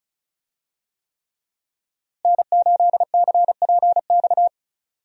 Morse code, heard as a keyed tone, about 36 wpm, 700 Hz. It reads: N8CPX
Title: USA Call Signs - 5 Characters - Repeated 3x - 35wpm